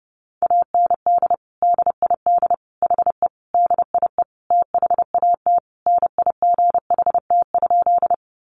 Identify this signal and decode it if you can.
Morse code, heard as a keyed tone, about 30 wpm, 700 Hz. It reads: AND DID HE DIE THAT NIGHT?
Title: Sentences from Top 1300 Words 30wpm